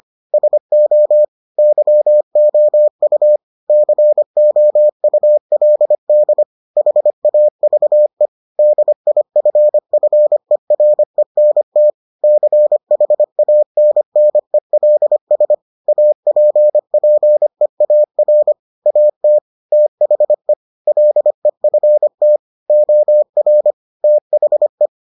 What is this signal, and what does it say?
Morse code, heard as a keyed tone, about 25 wpm, 600 Hz. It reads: SO YOU COULD HAVE DIFFERENT CHANNELS APPEAR AT THE LEFT OR THE